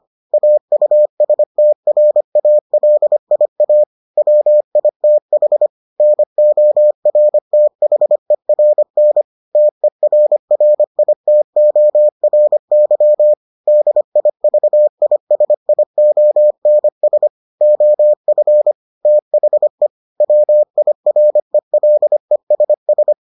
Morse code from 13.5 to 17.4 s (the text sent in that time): DIVISIONS